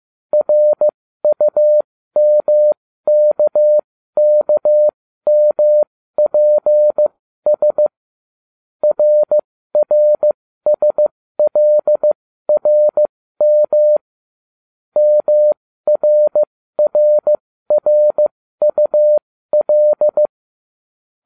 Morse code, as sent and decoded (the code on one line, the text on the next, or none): .-. ..- -- -.- -.- -- .--. ... / .-. .-. ... .-.. .-. -- / -- .-. .-. .-. ..- .-..
RUMKKMPS RRSLRM MRRRUL